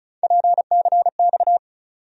Morse code, heard as a keyed tone, about 34 wpm, 700 Hz. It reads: PCX
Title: USA Call Sign Postfixes - 3 Letters 35wpm